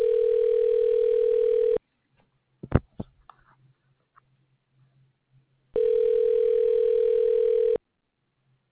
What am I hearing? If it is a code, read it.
Ringback tone (North America), 440 + 480 Hz.